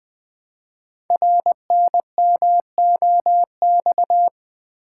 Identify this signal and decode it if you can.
Morse code, heard as a keyed tone, about 20 wpm, 700 Hz. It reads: RNMOX
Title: Code Groups - 5 Letters - 20wpm